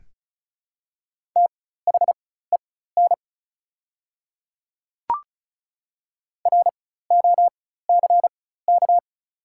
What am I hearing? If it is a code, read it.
Morse code, heard as a keyed tone, about 36 wpm, spaced as for 16 wpm, 700 Hz. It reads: THEN ROCK
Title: 4x Character Spacing - Top 500 Words - 35wpm